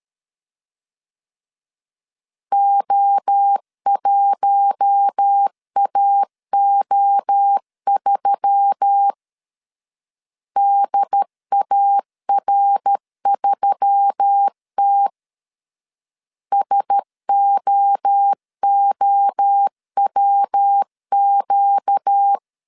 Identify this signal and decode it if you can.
Morse code, heard as a keyed tone, about 13 wpm, 800 Hz. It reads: O1AO3 DAR3T SOOWQ
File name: CW / DET / 1 6 1500C S B 12wpm 3431